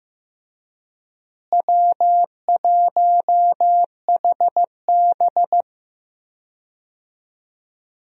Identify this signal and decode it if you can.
Morse code, heard as a keyed tone, about 15 wpm, 700 Hz. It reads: W1HB
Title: USA Call Signs - 4 Characters - Repeated 3x - 15wpm